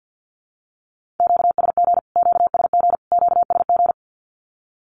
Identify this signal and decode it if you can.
Morse code, heard as a keyed tone, about 49 wpm, 700 Hz. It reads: QSG QSG QSG
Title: Q-Codes Repeated 3x 50wpm